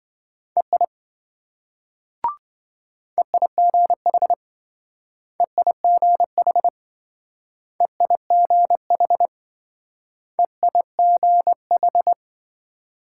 Morse code, heard as a keyed tone, about 26 wpm, 700 Hz. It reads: EI EIGH EIGH EIGH EIGH